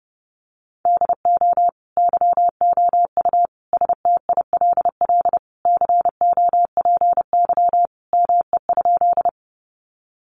Morse code, heard as a keyed tone, about 30 wpm, 700 Hz. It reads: DO YOU STILL COPY ME?
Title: Sentences from Top 1400 Words 30wpm